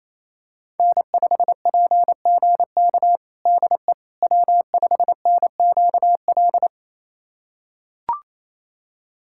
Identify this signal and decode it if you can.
Morse code, heard as a keyed tone, about 27 wpm, 700 Hz. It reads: N5PGK DE W5NQL
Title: USA Callsign DE Callsign - 5 Characters - 28wpm